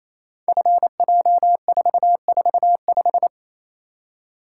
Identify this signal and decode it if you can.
Morse code, heard as a keyed tone, about 28 wpm, 700 Hz. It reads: FJ445
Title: Code Groups - 5 Letter:Numbers - 28wpm